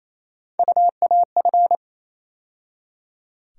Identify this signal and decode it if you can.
Morse code, heard as a keyed tone, about 27 wpm, 700 Hz. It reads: UAF
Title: USA Call Sign Postfixes - 3 Letters 28wpm